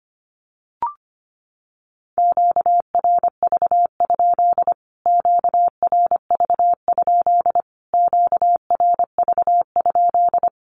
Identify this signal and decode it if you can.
Morse code, heard as a keyed tone, about 25 wpm, 700 Hz. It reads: QRV? QRV? QRV?
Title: Q-Codes Repeated 3x 25wpm